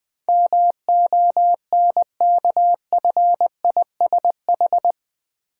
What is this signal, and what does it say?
Morse code, heard as a keyed tone, about 20 wpm, 700 Hz. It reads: MONKFISH
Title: Difficult Words with Hints - 20wpm